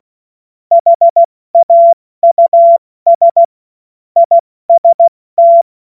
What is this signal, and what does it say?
Morse code, heard as a keyed tone, about 16 wpm, 680 Hz. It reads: HAUS IST
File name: Morsekurs / Witze / Witz 20. Januar 2026 70 F80